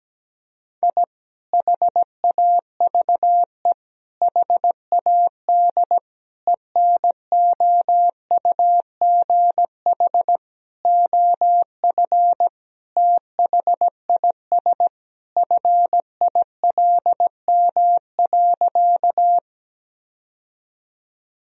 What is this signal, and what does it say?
Morse code, heard as a keyed tone, about 17 wpm, 700 Hz. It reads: I HAVE HAD ENOUGH OF THIS FILM.
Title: Sentences from Top 1000 Words - Review Part 3 - 17wpm